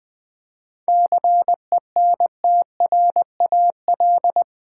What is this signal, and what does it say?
Morse code, heard as a keyed tone, about 20 wpm, 700 Hz. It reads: CENTRAL
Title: 7 Letter Words - First 3 Letters Given - 20wpm